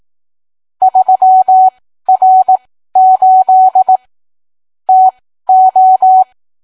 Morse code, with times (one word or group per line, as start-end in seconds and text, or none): 0.8-4.0 3R8
4.9-6.2 TO